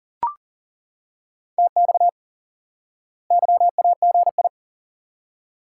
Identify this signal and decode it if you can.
Morse code, heard as a keyed tone, about 40 wpm, 700 Hz. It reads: TX YAGI
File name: IQE Course 3 - Lesson 89 - YAGI - Two Elements - 40wpm